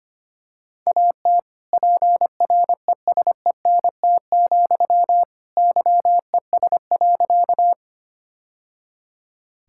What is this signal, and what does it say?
Morse code, heard as a keyed tone, about 25 wpm, 700 Hz. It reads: AT PRESENT, YES.
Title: Sentences from Top 1000 Words - Review Part 5 - 25wpm